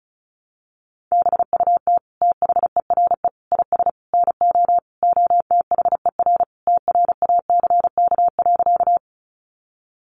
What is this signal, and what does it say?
Morse code, heard as a keyed tone, about 34 wpm, 700 Hz. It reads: BUT THERE IS NO OTHER TRACK.
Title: Sentences from Top 1100 Words 35wpm